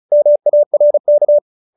Morse code, heard as a keyed tone, about 34 wpm, 570 Hz. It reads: MARK